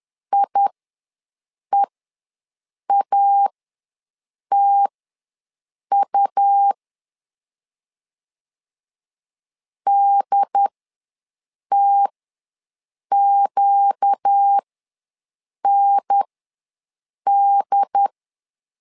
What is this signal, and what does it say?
Morse code, heard as a keyed tone, about 10 wpm, spaced as for 6 wpm, 800 Hz. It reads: IEATU DTQND
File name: LEZIONI / 4 Q ws 10wpm 2306